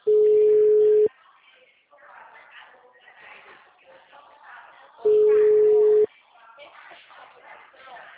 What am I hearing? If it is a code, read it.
Ringback tone (Europe, 425 Hz), with speech.
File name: out- -2002- - - .58227